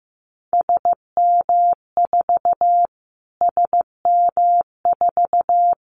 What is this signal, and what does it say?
Morse code, heard as a keyed tone, about 15 wpm, 700 Hz. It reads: SM4 SM4
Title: Callsign Prefixes for Top 25 DXCC - Repeated 3x - 15wpm